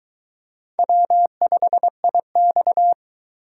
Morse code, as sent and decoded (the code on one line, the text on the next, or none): .-- ..... .. -..-
W5IX